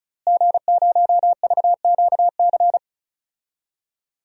Morse code, heard as a keyed tone, about 36 wpm, 700 Hz. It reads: G0VQC
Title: DXCC-100 Call Signs - 5 Characters - Repeated 2x - 35wpm